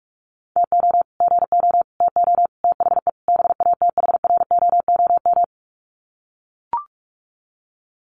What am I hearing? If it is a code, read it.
Morse code, heard as a keyed tone, about 44 wpm, 700 Hz. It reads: TO GO TO THE BATHROOM
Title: Sets of 5 Words Encore 45wpm